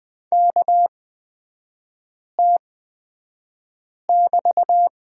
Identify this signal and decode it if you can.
Morse code, heard as a keyed tone, about 20 wpm, spaced as for 5 wpm, 700 Hz. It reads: KT<BT>
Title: 224 Lesson 36e - BT - Three Characters 20wpm